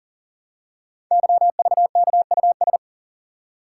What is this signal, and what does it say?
Morse code, heard as a keyed tone, about 39 wpm, 700 Hz. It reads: YVKUS
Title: Code Groups - 5 Letters - 40wpm